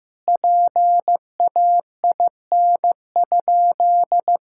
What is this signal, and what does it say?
Morse code, heard as a keyed tone, about 15 wpm, 700 Hz. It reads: PAIN?
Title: Sentences from Top 1600 Words 15wpm